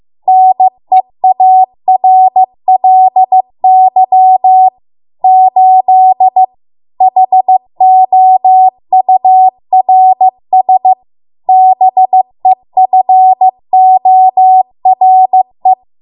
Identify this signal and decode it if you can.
Morse code, heard as a keyed tone, about 15 wpm, 750 Hz. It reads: NEARLY 8 HOURS BEFORE